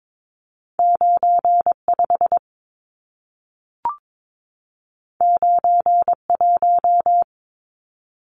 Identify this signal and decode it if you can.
Morse code, heard as a keyed tone, about 22 wpm, 700 Hz. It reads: 95 91
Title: Numbers - 2-Digit 22wpm